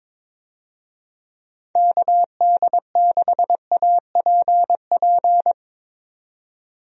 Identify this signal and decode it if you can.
Morse code, heard as a keyed tone, about 22 wpm, 700 Hz. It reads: KD6APP
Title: USA Call Signs - 6 Characters - 22wpm